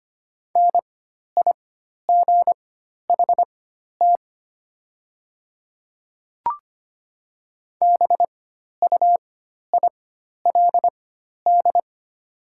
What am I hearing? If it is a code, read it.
Morse code, heard as a keyed tone, about 25 wpm, spaced as for 12 wpm, 700 Hz. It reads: NIGHT BUILD
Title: 4x Character Spacing - Top 500 Words - 25wpm